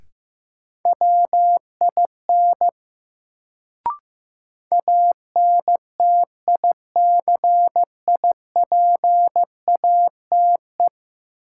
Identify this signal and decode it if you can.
Morse code, heard as a keyed tone, about 15 wpm, 700 Hz. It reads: WIN ANTICIPATE
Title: Sentences from Top 1500 Words - New Word Drill - 15wpm